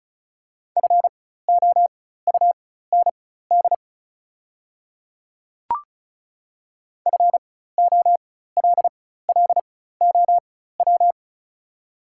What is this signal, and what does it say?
Morse code, heard as a keyed tone, about 36 wpm, spaced as for 16 wpm, 690 Hz. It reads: FOUND FOLLOW
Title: 4x Character Spacing - Top 500 Words - 35wpm